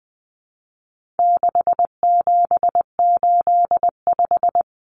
Morse code, heard as a keyed tone, about 20 wpm, 700 Hz. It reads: 6785
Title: Common Cut Numbers - 4-Digit 20wpm